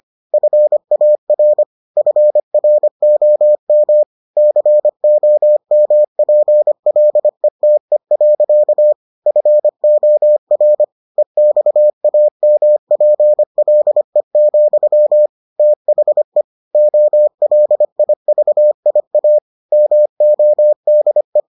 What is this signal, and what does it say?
Morse code, heard as a keyed tone, about 25 wpm, 600 Hz. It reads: FAR FROM COMPLETE. FOR EXAMPLE, THE OLIVIA MODE